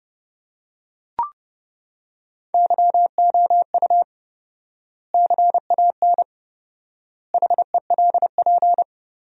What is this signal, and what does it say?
Morse code, heard as a keyed tone, about 30 wpm, 700 Hz. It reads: YOU CAN HELP
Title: 4x Word-Spacing - Sentences from Top 200 Words - 30wpm